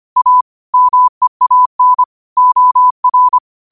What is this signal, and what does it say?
Morse code, heard as a keyed tone, about 25 wpm, 1000 Hz. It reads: A MEAN OR